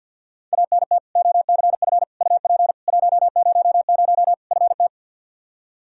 Morse code, heard as a keyed tone, about 49 wpm, 700 Hz. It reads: ANT OCF UP 100 FT
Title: Practice QSO - Exchange 2 - Info - 50wpm